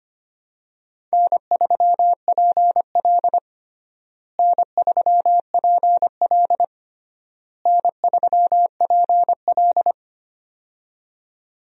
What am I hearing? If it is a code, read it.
Morse code, heard as a keyed tone, about 25 wpm, 700 Hz. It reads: N3PL N3PL N3PL
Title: USA Call Signs - 4 Characters - Repeated 3x - 25wpm